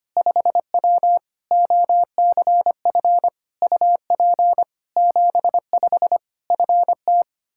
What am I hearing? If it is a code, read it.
Morse code, heard as a keyed tone, about 25 wpm, 700 Hz. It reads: 5W OCF UP 75 FT
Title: K1USN SST QSO Practice 25wpm